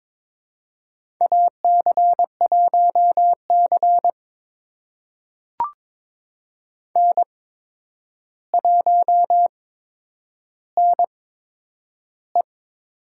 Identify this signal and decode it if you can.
Morse code, heard as a keyed tone, about 22 wpm, spaced as for 9 wpm, 700 Hz. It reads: AC1C N1NE
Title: USA Call Signs - 4 Characters - 8x Character Spacing - 22wpm